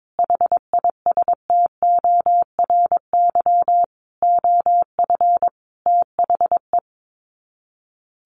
Morse code, heard as a keyed tone, about 22 wpm, 700 Hz. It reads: HISTORY OF THE
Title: Sets of 5 Words 22wpm